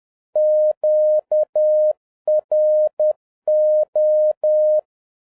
Morse code, heard as a keyed tone, about 10 wpm, 610 Hz. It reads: QRO
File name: 2011-07-mp3 / n41 EXCITING QRP CW OPERATION 10wpm